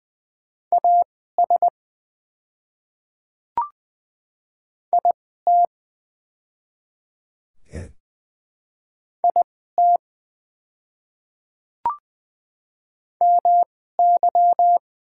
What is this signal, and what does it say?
Morse code, heard as a keyed tone, about 20 wpm, spaced as for 14 wpm, 700 Hz. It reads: AS IT IT MY
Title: 2x Character Spacing - Top 100 Words - 20wpm